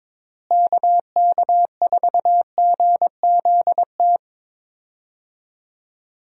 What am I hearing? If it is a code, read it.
Morse code, heard as a keyed tone, about 22 wpm, 700 Hz. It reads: KK4GZT
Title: USA Call Signs - 6 Characters - Repeated 3x - 22wpm